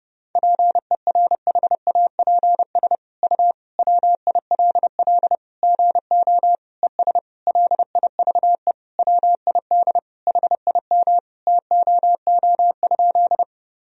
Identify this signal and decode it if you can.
Morse code, heard as a keyed tone, about 32 wpm, 700 Hz. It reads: PERHAPS U WILL GO ES LIVE WID HIM TOO?
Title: Sentences from Top 500 Words and Abbreviations 30wpm